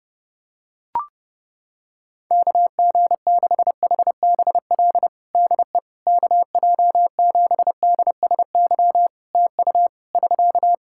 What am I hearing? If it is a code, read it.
Morse code, heard as a keyed tone, about 30 wpm, 700 Hz. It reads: KG6HBL DE KJ7DSY TU <SK>
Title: K1USN SST QSO Practice 30wpm